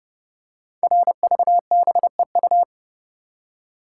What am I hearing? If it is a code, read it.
Morse code, heard as a keyed tone, about 30 wpm, 700 Hz. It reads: RVBEU
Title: Code Groups - 5 Letters - 30wpm